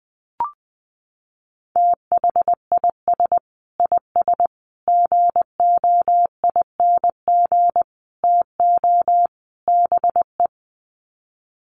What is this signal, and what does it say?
Morse code, heard as a keyed tone, about 20 wpm, 700 Hz. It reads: THIS IS GOING TO BE
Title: Sets of 5 Words 20wpm